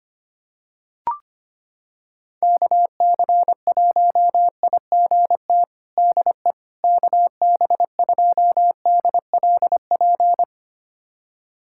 Morse code, heard as a keyed tone, about 25 wpm, 700 Hz. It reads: KC1IGT DE KB2DLP
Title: USA Callsign DE Callsign - 6 Characters - 25wpm